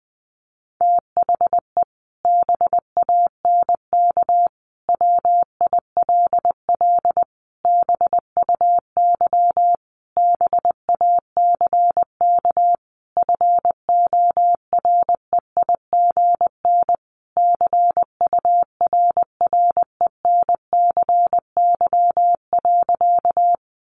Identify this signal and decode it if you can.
Morse code, heard as a keyed tone, about 20 wpm, 700 Hz. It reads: THE BANK WILL BUY BACK FOREIGN CURRENCY.
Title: Sentences from Top 1500 Words 20wpm